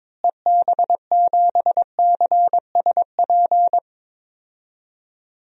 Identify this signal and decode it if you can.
Morse code, heard as a keyed tone, about 22 wpm, 700 Hz. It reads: EB7CSP
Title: DXCC-100 Call Signs - 6 Characters - Repeated 3x - 22wpm